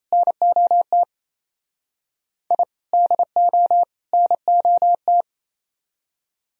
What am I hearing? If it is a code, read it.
Morse code, heard as a keyed tone, about 35 wpm, 700 Hz. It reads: NOT I DO NOT
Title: Speed-Racing - Sets of 3 Words 28wpm